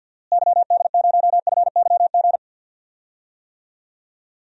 Morse code, heard as a keyed tone, about 48 wpm, 700 Hz. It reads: YD0FYG
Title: DXCC-100 Call Signs - 6 Characters - Repeated 2x - 50wpm